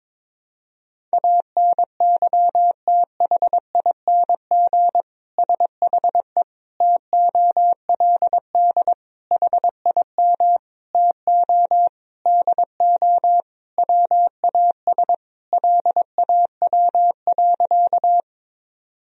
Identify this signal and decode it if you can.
Morse code, heard as a keyed tone, about 22 wpm, 700 Hz. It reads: ANYTHING SHE TOLD HIM TO DO WAS LAW.
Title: Sentences from Top 1000 Words - Review Part 3 - 22wpm